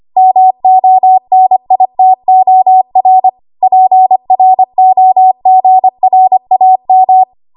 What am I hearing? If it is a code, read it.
Morse code, heard as a keyed tone, about 25 wpm, 750 Hz. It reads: MONITOR PROGRAM
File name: Archive / 25 WPM / 25WPM